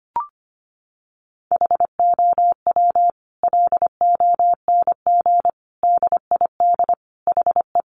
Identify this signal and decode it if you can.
Morse code, heard as a keyed tone, about 25 wpm, 700 Hz. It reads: HOW LONG DID HE
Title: Sentences from Top 1500 Words 25wpm